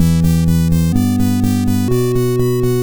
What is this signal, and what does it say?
There is no code.